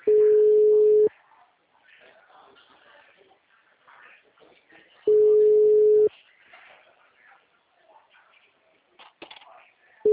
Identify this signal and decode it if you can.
Ringback tone (Europe, 425 Hz).